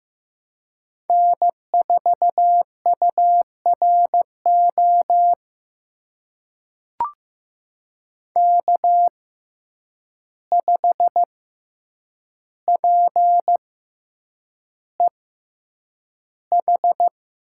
Morse code, heard as a keyed tone, about 15 wpm, spaced as for 8 wpm, 700 Hz. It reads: N4URO K5PEH
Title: USA Call Signs - 5 Characters - 6x Character Spacing - 15wpm